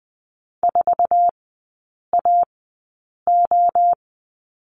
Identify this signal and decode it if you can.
Morse code, heard as a keyed tone, about 20 wpm, spaced as for 8 wpm, 700 Hz. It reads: 4AO
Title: 047 Lesson 9e - Number 4 - Three Characters 20wpm